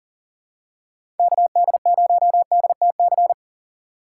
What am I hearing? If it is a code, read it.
Morse code, heard as a keyed tone, about 37 wpm, 700 Hz. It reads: KD0DTC
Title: USA Call Signs - 6 Characters - 3x Character Spacing - 40wpm